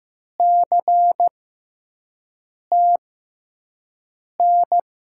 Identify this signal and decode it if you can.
Morse code, heard as a keyed tone, about 15 wpm, spaced as for 5 wpm, 700 Hz. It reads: CTN
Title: USA Call Signs - 6 Characters - 6x Character Spacing - 15wpm